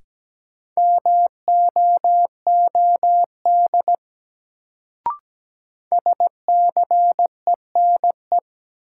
Morse code, heard as a keyed tone, about 17 wpm, 700 Hz. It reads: MOOD SCENE
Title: Sentences from Top 1100 Words - New Word Drill - 17wpm